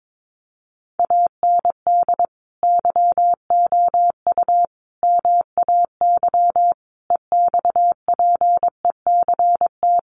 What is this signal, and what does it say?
Morse code, heard as a keyed tone, about 22 wpm, 700 Hz. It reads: AND YOU MAY EXPECT
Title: Sentences from Top 1500 Words 22wpm